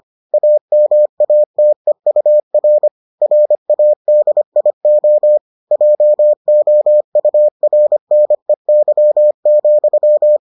Morse code, heard as a keyed tone, about 25 wpm, 600 Hz. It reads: AMATEUR RADIO JOURNEY,